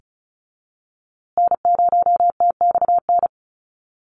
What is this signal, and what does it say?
Morse code, heard as a keyed tone, about 34 wpm, 700 Hz. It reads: N0TXN
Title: USA Call Signs Repeated 2x 35wpm